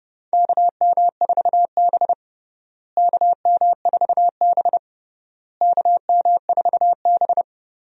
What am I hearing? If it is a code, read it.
Morse code, heard as a keyed tone, about 30 wpm, 700 Hz. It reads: KM4B KM4B KM4B
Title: USA Call Signs - 4 Characters - Repeated 3x - 30wpm